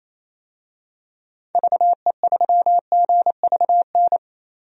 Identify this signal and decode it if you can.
Morse code, heard as a keyed tone, about 28 wpm, 700 Hz. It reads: VE3GVN
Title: DXCC-100 Call Signs - 6 Characters - 28wpm